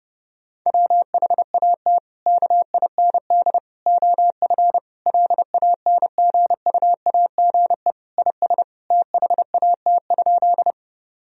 Morse code, heard as a keyed tone, about 30 wpm, 700 Hz. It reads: WHAT KIND OF LANGUAGE IS THAT?
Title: Sentences from Top 1000 Words - Review Part 3 - 30wpm